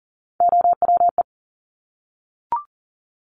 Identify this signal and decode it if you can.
Morse code, heard as a keyed tone, about 42 wpm, 700 Hz. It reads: OWE
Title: Top Three Letter Combinations in Words 40wpm